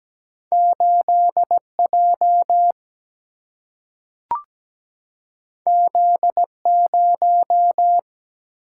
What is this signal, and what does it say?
Morse code, heard as a keyed tone, about 17 wpm, 700 Hz. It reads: 8J Z0
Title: Single-Letter Single-Number 17wpm